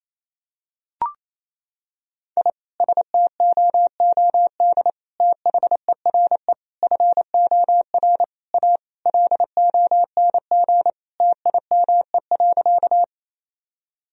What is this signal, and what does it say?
Morse code, heard as a keyed tone, about 27 wpm, 700 Hz. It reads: I STOOD THERE FOR A LONG TIME.
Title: Sentences from Top 1000 Words - Review Part 2 - 28wpm